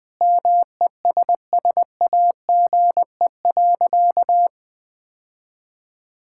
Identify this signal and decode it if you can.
Morse code, heard as a keyed tone, about 20 wpm, 700 Hz. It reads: MESSAGE.
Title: Sentences with 9-Word Phrases - 20wpm